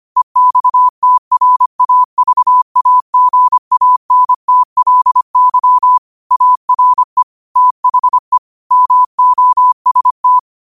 Morse code, heard as a keyed tone, about 25 wpm, 1000 Hz. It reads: EXTRAVAGANTLY ARE THE MOST